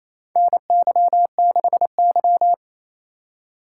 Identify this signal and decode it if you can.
Morse code, heard as a keyed tone, about 28 wpm, 700 Hz. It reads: NY6Y
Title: K1USN SST QSO Practice - 28wpm